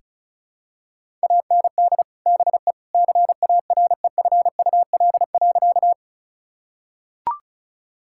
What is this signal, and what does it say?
Morse code, heard as a keyed tone, about 34 wpm, 700 Hz. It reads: AND BE CAREFUL.
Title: Sentences from Top 1600 Words 35wpm